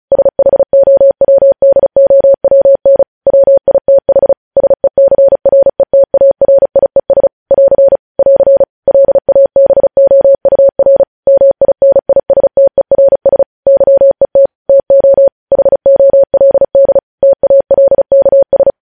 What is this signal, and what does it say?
Morse code, heard as a keyed tone, about 36 wpm, 550 Hz. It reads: SHOWDOWN WITH SECRETARIES <AR> <AR> LABOUR MINISTERS YET TO HOLD TALKS
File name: podcast / News headlines-1x-35x35